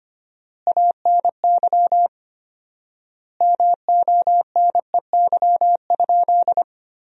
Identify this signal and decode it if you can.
Morse code, heard as a keyed tone, about 25 wpm, 700 Hz. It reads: ANY MONEY?
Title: 4x Word-Spacing - Sentences from Top 400 Words - 25wpm